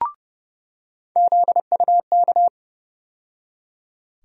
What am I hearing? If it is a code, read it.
Morse code, heard as a keyed tone, about 30 wpm, 700 Hz. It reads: ZUK
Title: USA Call Sign Postfixes - 3 Letters 30wpm